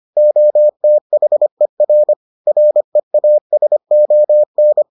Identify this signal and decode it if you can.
Morse code, heard as a keyed tone, about 25 wpm, 600 Hz. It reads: OTHER REASON